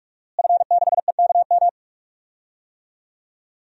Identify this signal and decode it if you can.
Morse code, heard as a keyed tone, about 48 wpm, 700 Hz. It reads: F6EKM
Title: DXCC-100 Call Signs - 5 Characters - Repeated 2x - 45wpm